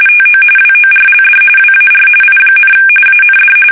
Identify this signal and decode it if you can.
RTTY, about 45.45 baud. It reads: TO WORLD WAR I, AND
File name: fsk850 / battleshipnj